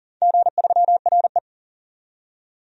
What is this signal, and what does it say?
Morse code, heard as a keyed tone, about 40 wpm, 700 Hz. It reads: G3RE
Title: DXCC-100 Call Signs - 4 Characters - 40wpm